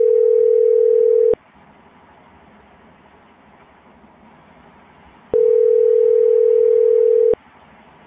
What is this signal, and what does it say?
Ringback tone (North America), 440 + 480 Hz.